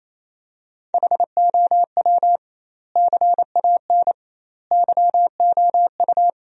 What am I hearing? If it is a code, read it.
Morse code, heard as a keyed tone, about 28 wpm, 700 Hz. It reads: HOW CAN YOU
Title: 2x Word-Spacing - Sentences from Top 200 Words - 28wpm